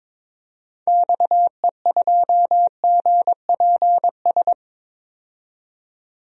Morse code, heard as a keyed tone, about 22 wpm, 700 Hz. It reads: XE2GPS
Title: DXCC-100 Call Signs - 6 Characters - 22wpm